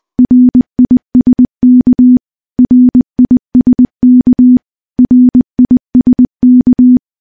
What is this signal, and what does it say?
Morse code, heard as a keyed tone, about 20 wpm, 250 Hz. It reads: RISK RISK RISK
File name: morse-training-20WPM